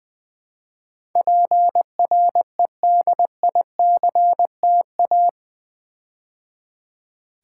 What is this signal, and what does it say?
Morse code, heard as a keyed tone, about 20 wpm, 700 Hz. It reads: PREDICTA
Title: Build A Word - 4 Letters At A Time - 11-Letter Words - 20wpm